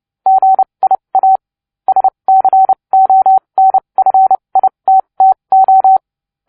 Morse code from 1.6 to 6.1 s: SCONFITTO